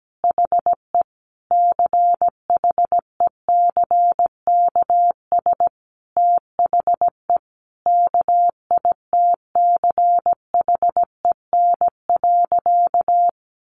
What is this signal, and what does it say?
Morse code, heard as a keyed tone, about 17 wpm, 700 Hz. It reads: HE CHECKS THE KITCHEN.
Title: Easy Phrases 17wpm